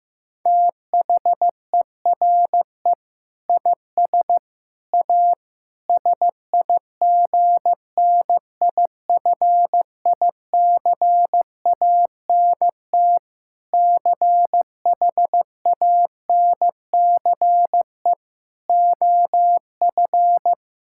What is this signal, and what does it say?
Morse code, heard as a keyed tone, about 16 wpm, 700 Hz. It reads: THERE IS A SIGNIFICANT CHANCE OF
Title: Sentences from Top 1000 Words - Review Part 6 - 15wpm